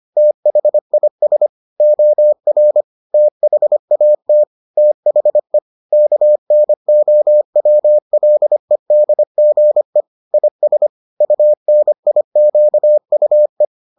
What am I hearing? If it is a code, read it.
Morse code, heard as a keyed tone, about 25 wpm, 600 Hz. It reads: THIS OR THAT THE KNOWLEDGE IS UNIQUE